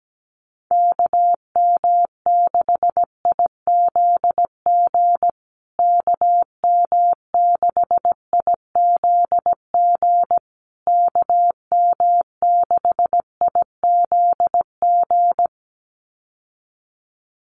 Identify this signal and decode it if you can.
Morse code, heard as a keyed tone, about 17 wpm, 700 Hz. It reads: KM6IZG KM6IZG KM6IZG
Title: USA Call Signs Repeated 3x 17wpm